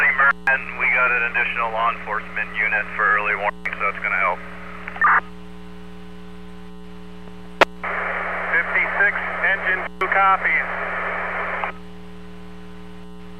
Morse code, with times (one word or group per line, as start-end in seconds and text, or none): none